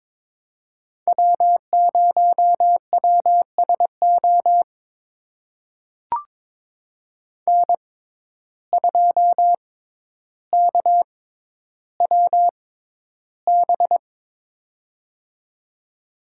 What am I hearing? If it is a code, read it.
Morse code, heard as a keyed tone, about 22 wpm, spaced as for 11 wpm, 700 Hz. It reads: W0WSO N2KWB
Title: USA Call Signs - 5 Characters - 6x Character Spacing - 22wpm